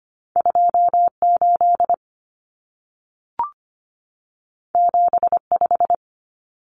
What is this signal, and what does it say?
Morse code, heard as a keyed tone, about 25 wpm, 700 Hz. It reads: 28 75
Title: Common Cut Numbers - 2-Digit 25wpm